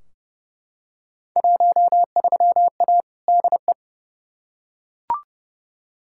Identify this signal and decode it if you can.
Morse code, heard as a keyed tone, about 30 wpm, 700 Hz. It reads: 13A DE
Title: Field Day Exchanges Encore - 30wpm